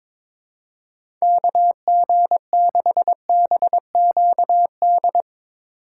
Morse code, heard as a keyed tone, about 22 wpm, 700 Hz. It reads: KG6BQD
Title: USA Call Signs - 6 Characters - 4x Character Spacing - 22wpm